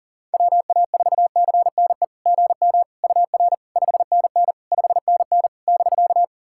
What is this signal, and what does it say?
Morse code, heard as a keyed tone, about 37 wpm, 690 Hz. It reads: WA4CNE GM UR 5NN 5NN <BK>
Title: SOTA QSO Practice 40wpm